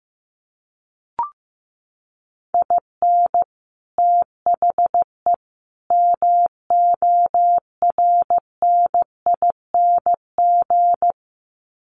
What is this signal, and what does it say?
Morse code, heard as a keyed tone, about 15 wpm, 700 Hz. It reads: IN THE MORNING
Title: Sets of 3 Words 15wpm